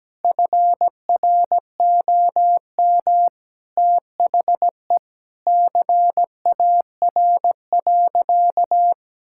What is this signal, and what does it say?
Morse code, heard as a keyed tone, about 17 wpm, 700 Hz. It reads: FROM THE CAR.
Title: Sentences from Top 1000 Words - Review Part 5 - 17wpm